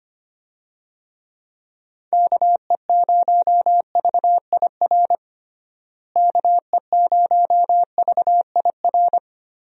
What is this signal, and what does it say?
Morse code, heard as a keyed tone, about 25 wpm, 700 Hz. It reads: KE0VIR KE0VIR
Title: USA Call Signs - 6 Characters - Repeated 3x - 25wpm